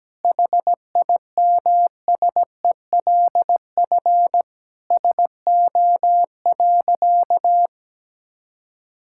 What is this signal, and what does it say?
Morse code, heard as a keyed tone, about 17 wpm, 700 Hz. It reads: HIMSELF SO.
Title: Sentences from Top 1600 Words 17wpm